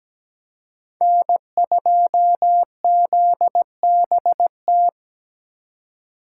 Morse code, heard as a keyed tone, about 20 wpm, 700 Hz. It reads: N2ZBT
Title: Speed Racing - USA 1x3 Call Signs 17wpm